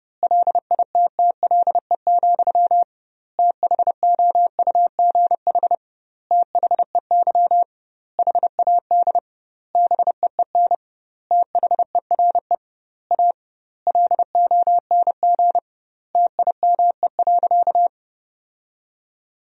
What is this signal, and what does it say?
Morse code, heard as a keyed tone, about 30 wpm, 700 Hz. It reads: LITTLE, THOUGH THEY HAD BEEN THERE A LONG TIME.
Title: 2x Word-Spacing - Sentences from Top 400 Words - 30wpm